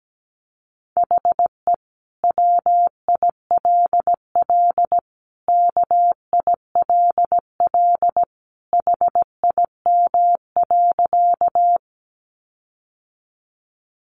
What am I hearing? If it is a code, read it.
Morse code, heard as a keyed tone, about 17 wpm, 700 Hz. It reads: HE WILL KILL HIM.
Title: Sentences from Top 1400 Words 17wpm